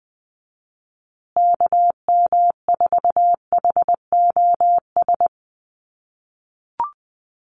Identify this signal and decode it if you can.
Morse code, heard as a keyed tone, about 20 wpm, 700 Hz. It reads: KM4HOS
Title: USA Call Signs Repeated 3x 20wpm